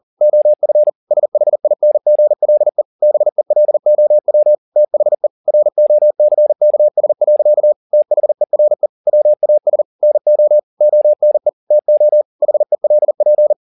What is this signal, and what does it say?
Morse code, heard as a keyed tone, about 40 wpm, 600 Hz. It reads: OF SHINGLE BELOW THE ROCKS. THERE WAS NO ONE TO HELP
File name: the old man and the sea 2 40wpm